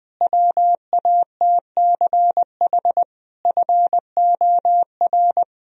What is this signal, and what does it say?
Morse code, heard as a keyed tone, about 21 wpm, 700 Hz. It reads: WATCH FOR
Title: Sentences from Top 1000 Words - Review Part 1 - 20wpm